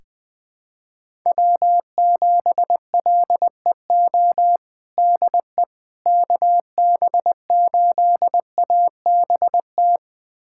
Morse code, heard as a keyed tone, about 20 wpm, 700 Hz. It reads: W7LEO DE KB8ABT
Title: USA Callsign DE Callsign 20wpm